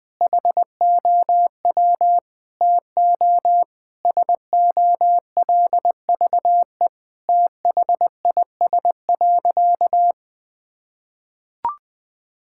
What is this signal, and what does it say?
Morse code, heard as a keyed tone, about 21 wpm, 700 Hz. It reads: HOW TO SOLVE THIS.
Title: Sentences with 5-Word Phrases - 20wpm